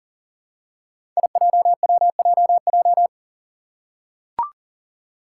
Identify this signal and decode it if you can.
Morse code, heard as a keyed tone, about 40 wpm, 700 Hz. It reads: IJWJJ
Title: Five Challenging Letter-Number Combinations 40wpm